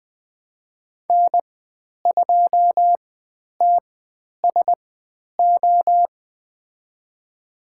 Morse code, heard as a keyed tone, about 20 wpm, spaced as for 10 wpm, 700 Hz. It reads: N2TSO
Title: 095 Lesson 16g - Letter c - Call Signs 20wpm